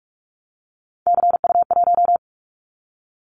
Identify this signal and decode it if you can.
Morse code, heard as a keyed tone, about 44 wpm, 700 Hz. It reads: CU1
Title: Callsign Prefixes for Top 100 DXCC - 45wpm